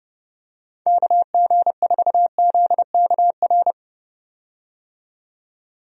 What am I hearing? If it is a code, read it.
Morse code, heard as a keyed tone, about 30 wpm, 700 Hz. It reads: KG4ZKR
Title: USA Call Signs - 6 Characters - 30wpm